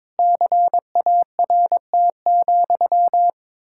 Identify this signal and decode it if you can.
Morse code, heard as a keyed tone, about 22 wpm, 700 Hz. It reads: CART,
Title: Pangram Challenge v2 - 22wpm